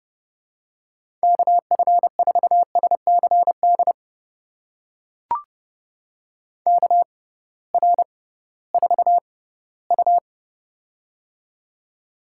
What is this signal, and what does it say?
Morse code, heard as a keyed tone, about 30 wpm, spaced as for 17 wpm, 700 Hz. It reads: KF4SCD KR4U
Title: USA Call Signs - 6x Character Spacing - 30wpm